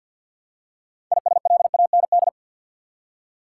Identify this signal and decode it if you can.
Morse code, heard as a keyed tone, about 53 wpm, 700 Hz. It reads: ISLAND
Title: Six Letter Words - 50wpm